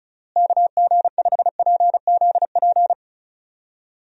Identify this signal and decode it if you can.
Morse code, heard as a keyed tone, about 34 wpm, 700 Hz. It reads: KG5PZP
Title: USA Call Signs - 6 Characters - Repeated 3x - 35wpm